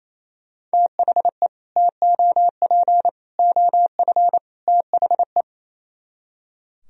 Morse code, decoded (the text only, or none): THE TOP OF THE